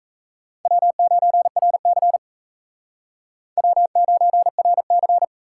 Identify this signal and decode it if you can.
Morse code, heard as a keyed tone, about 34 wpm, 700 Hz. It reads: W9RC W9RC
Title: Speed Racing - USA 1x2 Call Signs 28wpm